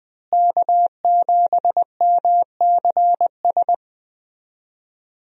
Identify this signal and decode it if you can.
Morse code, heard as a keyed tone, about 20 wpm, 700 Hz. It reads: K7MCS
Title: USA Call Signs - 5 Characters - Repeated 3x - 20wpm